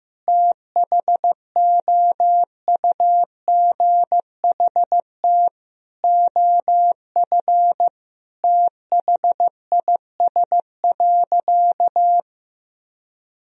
Morse code, heard as a keyed tone, about 15 wpm, 700 Hz. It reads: THOUGHT OF THIS.
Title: Sentences from Top 1000 Words - Review Part 1 - 15wpm